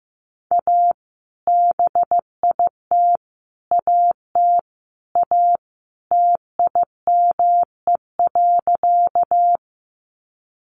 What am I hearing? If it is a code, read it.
Morse code, heard as a keyed tone, about 15 wpm, 700 Hz. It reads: A BIT AT A TIME.
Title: Sentences from Top 900 Words 15wpm